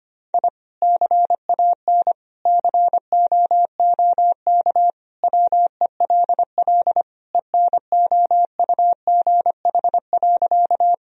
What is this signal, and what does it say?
Morse code, heard as a keyed tone, about 25 wpm, 700 Hz. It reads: I CAN COOK WELL ENOUGH.
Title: Sentences from Top 1000 Words - Review Part 4 - 25wpm